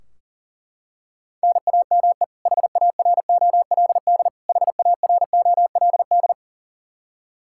Morse code, heard as a keyed tone, about 40 wpm, 700 Hz. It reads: NAME HAROLD HAROLD
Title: Practice QSO - Exchange 1 - Beginning - 40wpm